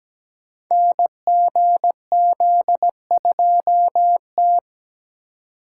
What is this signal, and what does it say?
Morse code, heard as a keyed tone, about 17 wpm, 700 Hz. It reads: NGZ2T
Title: Code Groups - 5 Letter:Numbers - 17wpm